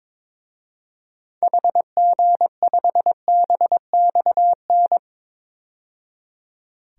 Morse code, heard as a keyed tone, about 22 wpm, 700 Hz. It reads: HG5BXN
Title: DXCC-100 Call Signs - 6 Characters - 22wpm